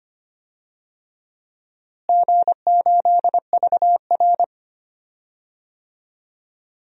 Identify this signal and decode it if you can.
Morse code, heard as a keyed tone, about 25 wpm, 700 Hz. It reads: G8VR
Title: DXCC-100 Call Signs - 4 Characters - Repeated 2x - 25wpm